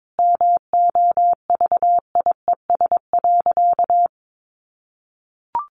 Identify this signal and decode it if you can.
Morse code, heard as a keyed tone, about 22 wpm, 700 Hz. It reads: MOVIES.
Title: Additional Easy Phrases 22wpm